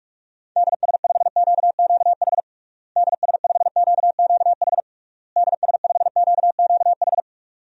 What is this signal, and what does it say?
Morse code, heard as a keyed tone, about 41 wpm, 700 Hz. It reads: DS5QQH DS5QQH DS5QQH
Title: DXCC-100 Call Signs - 6 Characters - Repeated 3x - 45wpm